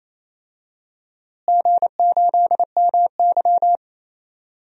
Morse code, heard as a keyed tone, about 28 wpm, 700 Hz. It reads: G8MY
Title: DXCC-100 Call Signs - 4 Characters - Repeated 3x - 28wpm